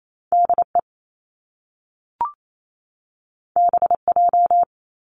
Morse code, heard as a keyed tone, about 28 wpm, 700 Hz. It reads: DE BJ
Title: Top Two Letter Combinations in Words 28wpm